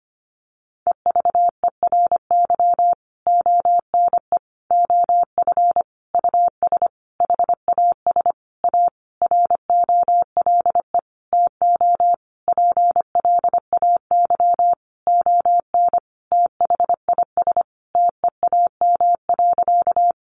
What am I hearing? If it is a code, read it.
Morse code, heard as a keyed tone, about 25 wpm, 700 Hz. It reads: EVERY ONE OF US HAS A ROLE TO PLAY ON THIS TEAM.
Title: Sentences from Top 1500 Words 25wpm